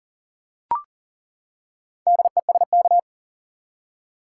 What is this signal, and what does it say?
Morse code, heard as a keyed tone, about 40 wpm, 700 Hz. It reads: DESK
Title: Four Letter Words - 40wpm